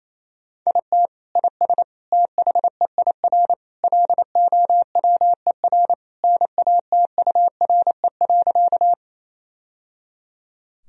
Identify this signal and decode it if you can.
Morse code, heard as a keyed tone, about 28 wpm, 700 Hz. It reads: IT IS THEIR LOWER NATURE.
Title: Sentences from Top 1000 Words - Review Part 4 - 28wpm